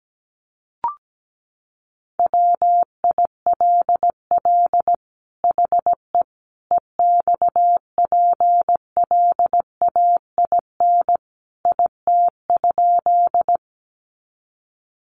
Morse code, heard as a keyed tone, about 17 wpm, 700 Hz. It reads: WILL HE EXPLAIN IT?
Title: Sentences from Top 1100 Words 17wpm